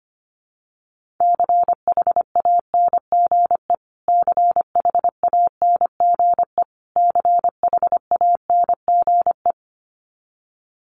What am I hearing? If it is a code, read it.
Morse code, heard as a keyed tone, about 25 wpm, 700 Hz. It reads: CHANGE CHANGE CHANGE
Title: Top 500 Words Repeated 25wpm